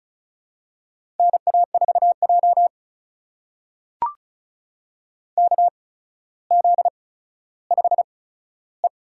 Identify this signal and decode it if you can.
Morse code, heard as a keyed tone, about 36 wpm, spaced as for 15 wpm, 700 Hz. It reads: NA4J KZ5E
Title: USA Call Signs - 4 Characters - 8x Character Spacing - 35wpm